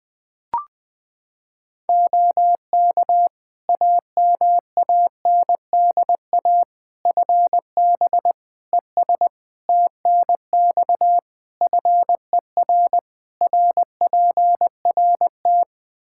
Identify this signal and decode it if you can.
Morse code, heard as a keyed tone, about 21 wpm, 700 Hz. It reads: OK AMANDA FB ES TNX FER RPRT
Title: K1USN SST QSO Practice 20wpm